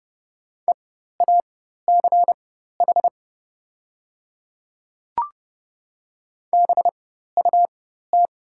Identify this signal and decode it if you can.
Morse code, heard as a keyed tone, about 30 wpm, spaced as for 14 wpm, 700 Hz. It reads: EACH BUT
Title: 4x Character Spacing - Top 200 Words - 30wpm